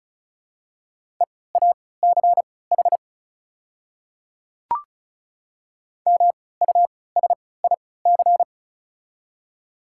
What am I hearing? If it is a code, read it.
Morse code, heard as a keyed tone, about 36 wpm, spaced as for 20 wpm, 700 Hz. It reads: EACH MUSIC
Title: 3x Character Spacing - Top 300 Words - 35wpm